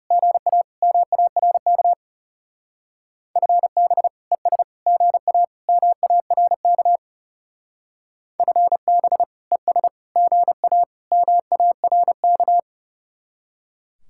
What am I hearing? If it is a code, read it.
Morse code, heard as a keyed tone, about 35 wpm, 700 Hz. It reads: GA MARK FB ES GA MARK FB ES GA MARK
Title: Speed-Racing Common QSO Phrases Encore - 30wpm